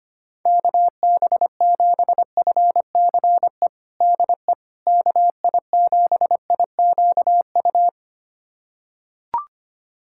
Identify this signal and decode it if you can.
Morse code, heard as a keyed tone, about 25 wpm, 700 Hz. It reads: KB7FCE DE KI7IQU
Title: USA Callsign DE Callsign - 6 Characters - 25wpm